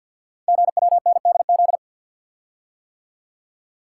Morse code, heard as a keyed tone, about 51 wpm, 700 Hz. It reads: GWNDZ
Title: Code Groups - 5 Letters - 50wpm